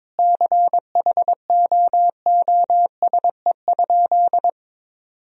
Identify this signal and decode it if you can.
Morse code, heard as a keyed tone, about 22 wpm, 700 Hz. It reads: CHOOSE?
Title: Sentences from Top 1000 Words - Review Part 4 - 22wpm